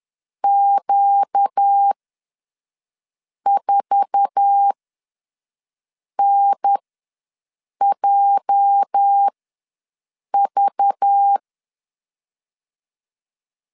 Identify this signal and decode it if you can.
Morse code, heard as a keyed tone, about 10 wpm, spaced as for 5 wpm, 800 Hz. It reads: Q4NJV
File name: LEZIONI / 8 YJ45 / 8 YJ4 ws 10wpm 4020